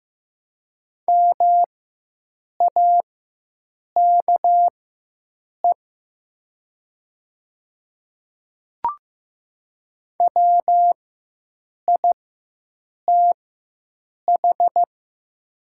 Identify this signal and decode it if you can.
Morse code, heard as a keyed tone, about 15 wpm, spaced as for 7 wpm, 700 Hz. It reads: MAKE WITH
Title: 4x Character Spacing - Top 100 Words - 15wpm